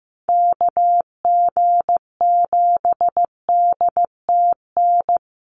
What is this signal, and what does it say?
Morse code, heard as a keyed tone, about 15 wpm, 700 Hz. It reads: KG7DTN
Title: USA Call Signs Repeated 2x 15wpm